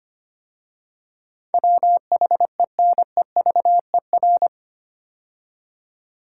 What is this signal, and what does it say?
Morse code, heard as a keyed tone, about 25 wpm, 690 Hz. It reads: WHENEVER
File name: 8 Letter Words - First 3 Letters Given - 25wpm